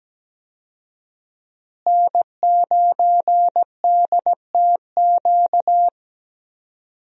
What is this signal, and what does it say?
Morse code, heard as a keyed tone, about 17 wpm, 700 Hz. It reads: N9DTQ
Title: USA Call Signs - 5 Characters - Repeated 3x - 17wpm